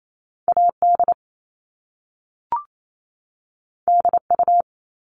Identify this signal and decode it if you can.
Morse code, heard as a keyed tone, about 28 wpm, 700 Hz. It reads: AD DU
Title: Top Two Letter Combinations in Words 28wpm